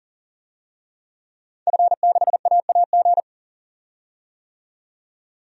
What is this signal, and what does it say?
Morse code, heard as a keyed tone, about 40 wpm, 700 Hz. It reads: F6AAG
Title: DXCC-100 Call Signs - 5 Characters - 40wpm